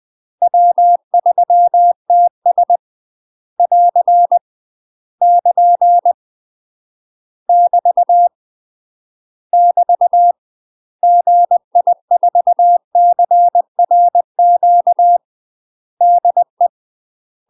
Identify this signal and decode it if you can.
Morse code, heard as a keyed tone, about 20 wpm, 700 Hz. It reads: W3TS <AR> <KN> <BT> <BT> GI4CRQ DE
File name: UK MASTER QSOs TRAINING / 20WPM / UK MASTER QSOs 0031